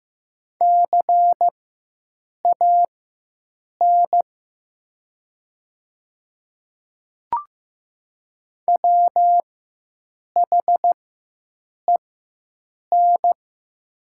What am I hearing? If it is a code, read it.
Morse code, heard as a keyed tone, about 15 wpm, spaced as for 7 wpm, 700 Hz. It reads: CAN WHEN
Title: 4x Character Spacing - Top 300 Words - 15wpm